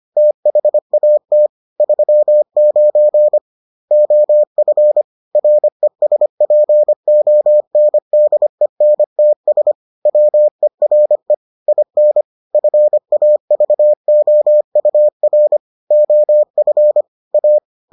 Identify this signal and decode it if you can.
Morse code, heard as a keyed tone, about 25 wpm, 600 Hz. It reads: THAT 39 OF RESPONDENTS WERE IN FAVOUR OF A